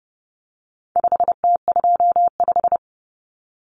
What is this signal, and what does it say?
Morse code, heard as a keyed tone, about 30 wpm, 700 Hz. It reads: 5T25
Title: Common Cut Numbers - 4-Digit 30wpm